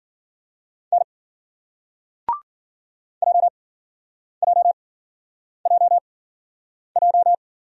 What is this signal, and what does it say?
Morse code, heard as a keyed tone, about 53 wpm, spaced as for 9 wpm, 700 Hz. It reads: N JJJJ